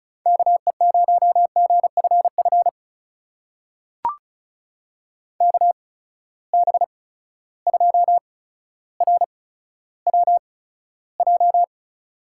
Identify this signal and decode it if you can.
Morse code, heard as a keyed tone, about 36 wpm, spaced as for 15 wpm, 700 Hz. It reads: KE0GFF KB2RWJ
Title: USA Call Signs - 6 Characters - 8x Character Spacing - 35wpm